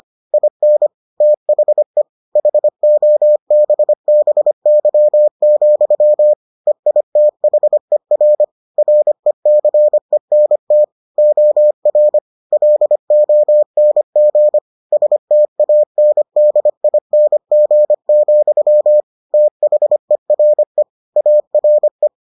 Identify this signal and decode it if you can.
Morse code, heard as a keyed tone, about 25 wpm, 600 Hz. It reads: IN THE HOBBY, EITHER RECENT OR LONG STANDING, THERE ARE